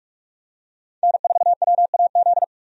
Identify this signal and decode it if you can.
Morse code, heard as a keyed tone, about 46 wpm, 700 Hz. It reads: N4WAZ
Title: USA Call Signs - 5 Characters - 45wpm